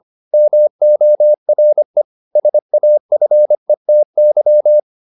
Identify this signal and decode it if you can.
Morse code, heard as a keyed tone, about 25 wpm, 600 Hz. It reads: MORE SAFETY